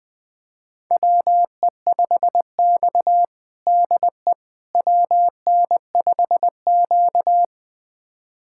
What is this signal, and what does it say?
Morse code, heard as a keyed tone, about 20 wpm, 700 Hz. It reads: WE5X DE WN5Q
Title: USA Callsign DE Callsign - 4 Characters - 20wpm